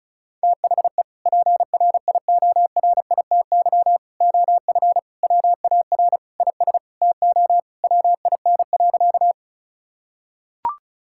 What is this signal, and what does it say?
Morse code, heard as a keyed tone, about 36 wpm, 700 Hz. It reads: THE PRIORITY OF WAR IS TO WIN.
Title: Sentences from Top 1600 Words 35wpm